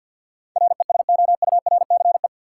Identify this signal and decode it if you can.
Morse code, heard as a keyed tone, about 53 wpm, 700 Hz. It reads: RESOURCE
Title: Sentences from Top 1300 Words - New Word Drill Alt - 50wpm